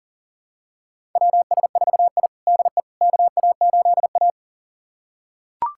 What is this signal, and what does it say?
Morse code, heard as a keyed tone, about 40 wpm, 700 Hz. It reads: WS4I DE KA8A
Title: USA Callsign DE Callsign - 4 Characters - 40wpm